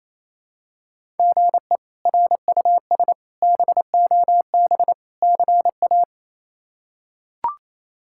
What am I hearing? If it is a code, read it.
Morse code, heard as a keyed tone, about 28 wpm, 700 Hz. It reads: GE RUS BOB CA
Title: K1USN SST QSO Practice - 28wpm